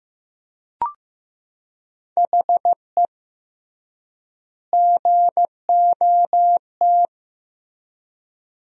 Morse code, heard as a keyed tone, about 15 wpm, 700 Hz. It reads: HE GOT
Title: 3x Word-Spacing - Sentences from Top 400 Words - 15wpm